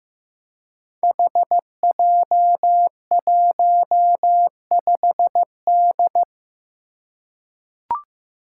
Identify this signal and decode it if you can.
Morse code, heard as a keyed tone, about 15 wpm, 700 Hz. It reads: HJ15D
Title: Five Challenging Letter-Number Combinations 15wpm